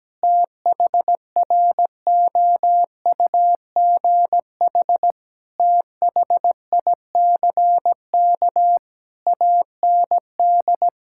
Morse code, heard as a keyed tone, about 17 wpm, 700 Hz. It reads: THROUGH THICK AND
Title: Sentences from Top 1600 Words 17wpm